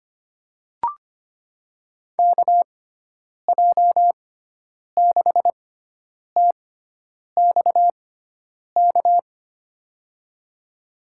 Morse code, heard as a keyed tone, about 25 wpm, spaced as for 9 wpm, 700 Hz. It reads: KJ6TXK
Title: USA Call Signs - 6 Characters - 6x Character Spacing - 25wpm